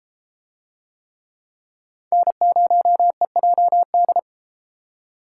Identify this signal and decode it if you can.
Morse code, heard as a keyed tone, about 32 wpm, 700 Hz. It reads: N0EJD